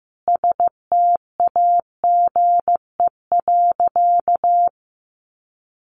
Morse code, heard as a keyed tone, about 15 wpm, 700 Hz. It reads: STAGE.
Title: Sentences from Top 1500 Words 15wpm